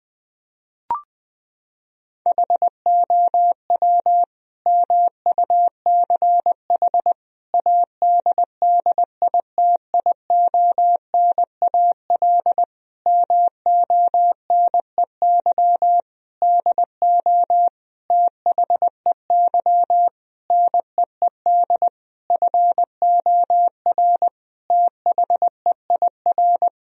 Morse code, heard as a keyed tone, about 20 wpm, 700 Hz. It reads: HOW MUCH ADDITIONAL MONEY DO THEY NEED FOR THEIR
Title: Sentences from Top 1000 Words - Review Part 4 - 20wpm